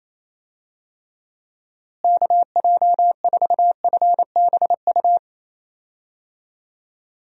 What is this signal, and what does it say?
Morse code, heard as a keyed tone, about 28 wpm, 700 Hz. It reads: KJ4FBU
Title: USA Call Signs - 6 Characters - Repeated 2x - 28wpm